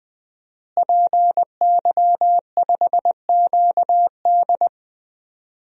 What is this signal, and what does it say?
Morse code, heard as a keyed tone, about 20 wpm, 700 Hz. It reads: PY5QD
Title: DXCC-100 Call Signs - 5 Characters - Repeated 3x - 20wpm